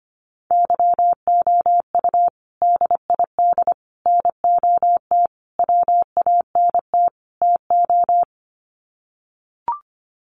Morse code, heard as a keyed tone, about 25 wpm, 700 Hz. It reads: YOU DID NOT WANT TO
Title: Sets of 5 Words Encore2 25wpm